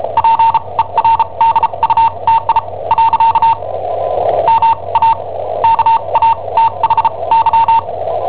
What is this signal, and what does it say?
Morse code, heard as a keyed tone, about 34 wpm, spaced as for 26 wpm, 910 Hz. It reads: PERDUTI . MA KATHY
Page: Cassetta esercizi preparata da I7OHP per IW5ELP